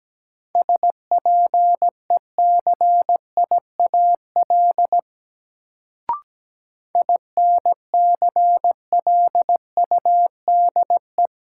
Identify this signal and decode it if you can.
Morse code, heard as a keyed tone, about 17 wpm, 700 Hz. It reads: SPECIAL INCLUDE
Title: Sentences from Top 1000 Words - Review Part 3 - New Word Drill - 17wpm